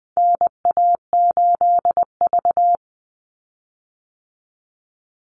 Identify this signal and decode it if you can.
Morse code, heard as a keyed tone, about 20 wpm, 700 Hz. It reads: NA8V
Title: Call Signs Repeated 3x - 20wpm